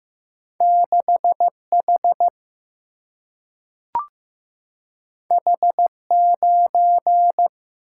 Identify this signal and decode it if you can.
Morse code, heard as a keyed tone, about 15 wpm, 700 Hz. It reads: 6 H H 9
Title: Single-Letter Single-Number 15wpm